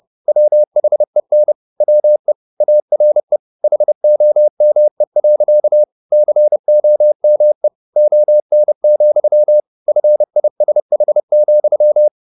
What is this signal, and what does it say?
Morse code, heard as a keyed tone, about 30 wpm, 600 Hz. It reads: WHEN WE ARE HOME. COME ON, FISH,